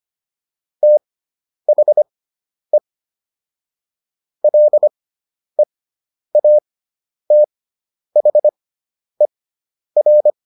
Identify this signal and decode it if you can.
Morse code, heard as a keyed tone, about 25 wpm, spaced as for 10 wpm, 600 Hz. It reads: THE LEATHER